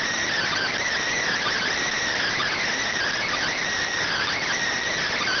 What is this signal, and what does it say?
Emergency Alert System (SAME) tones, heard as harsh buzzing bursts. It is heard from 0.5 to 4.5 s.
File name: findings / CT CORONAL 710mm 10mm scout.gen-35 rx